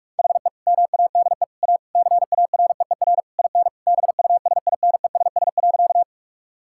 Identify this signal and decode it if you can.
Morse code, heard as a keyed tone, about 41 wpm, 700 Hz. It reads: HE MADE A CAREER IN BUSINESS.
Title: Sentences from Top 1000 Words - Review Part 6 - 45wpm